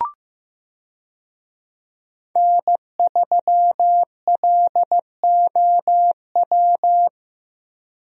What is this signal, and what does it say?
Morse code, heard as a keyed tone, about 15 wpm, 700 Hz. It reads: N3LOW
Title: USA Call Signs - 5 Characters - Repeated 3x - 15wpm